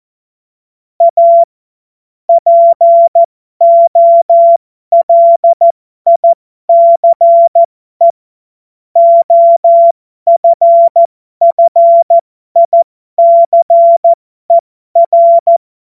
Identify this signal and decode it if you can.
Morse code, heard as a keyed tone, about 14 wpm, spaced as for 12 wpm, 680 Hz. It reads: A POLICE OFFICER
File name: Morsekurs / Witze